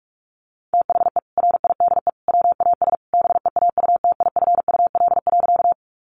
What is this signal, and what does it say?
Morse code, heard as a keyed tone, about 44 wpm, 700 Hz. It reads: THE RIDE WAS BEAUTIFUL.
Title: Sentences from Top 1200 Words 45wpm